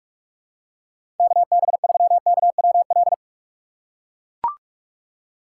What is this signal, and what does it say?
Morse code, heard as a keyed tone, about 44 wpm, 700 Hz. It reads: KB3KWL
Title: USA Call Signs - 6 Characters - 6x Character Spacing - 45wpm